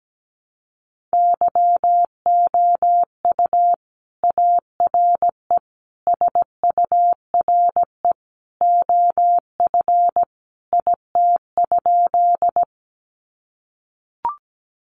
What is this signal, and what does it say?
Morse code, heard as a keyed tone, about 17 wpm, 700 Hz. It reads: YOU ARE SURE OF IT?
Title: Sentences from Top 300 Words 17wpm